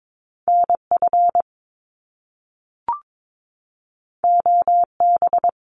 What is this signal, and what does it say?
Morse code, heard as a keyed tone, about 22 wpm, 700 Hz. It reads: N F O B
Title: Top Two Letter Combinations in Words 22wpm